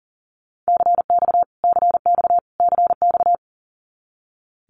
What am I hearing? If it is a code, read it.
Morse code, heard as a keyed tone, about 40 wpm, 700 Hz. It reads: CX CX CX
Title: Abbreviations Repeated 3x - 40wpm